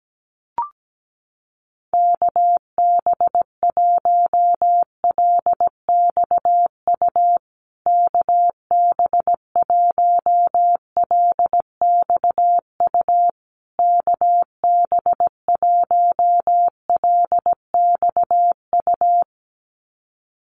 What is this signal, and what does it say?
Morse code, heard as a keyed tone, about 17 wpm, 700 Hz. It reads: KB1LXU KB1LXU KB1LXU
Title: USA Call Signs Repeated 3x 17wpm